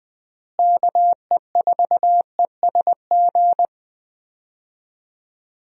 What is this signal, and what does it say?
Morse code, heard as a keyed tone, about 20 wpm, 700 Hz. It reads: KE4ESG
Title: USA Call Signs - 6 Characters - Repeated 3x - 20wpm